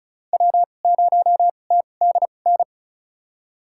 Morse code, heard as a keyed tone, about 36 wpm, spaced as for 25 wpm, 700 Hz. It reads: W0TDN
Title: USA Call Signs - 5 Characters - 2x Character Spacing - 35wpm